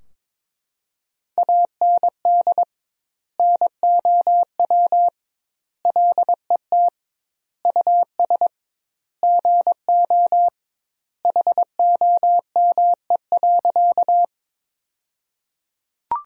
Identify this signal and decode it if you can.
Morse code, heard as a keyed tone, about 22 wpm, 700 Hz. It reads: AND NOW LET US GO HOME.
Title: 2x Word-Spacing - Sentences from Top 300 Words - 22wpm